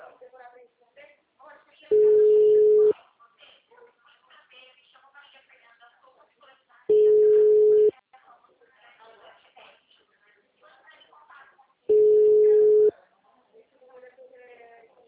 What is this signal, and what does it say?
Ringback tone (Europe, 425 Hz), with speech.
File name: out- -2005- - - .26